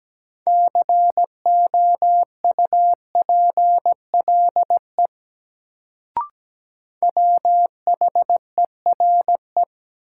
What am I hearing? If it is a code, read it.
Morse code, heard as a keyed tone, about 17 wpm, 700 Hz. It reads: COUPLE WHERE
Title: Sets of 2 Words - New Word Drill - 17wpm